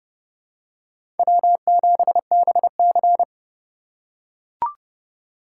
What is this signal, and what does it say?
Morse code, heard as a keyed tone, about 32 wpm, 700 Hz. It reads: W7BC
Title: USA Call Signs - 4 Characters - 3x Character Spacing - 30wpm